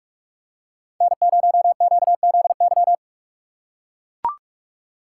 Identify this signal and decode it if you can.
Morse code, heard as a keyed tone, about 44 wpm, 700 Hz. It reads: N0QZY
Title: USA Call Signs - 5 Characters - 6x Character Spacing - 45wpm